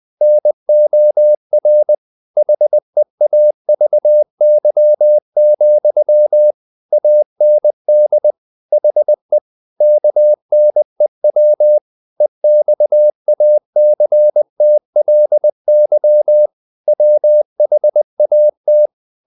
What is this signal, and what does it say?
Morse code, heard as a keyed tone, about 20 wpm, 600 Hz. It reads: NOR HEAVY, AND HE KNEW EXACTLY WHAT
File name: the old man and the sea / the old man and the sea part 1 20wpm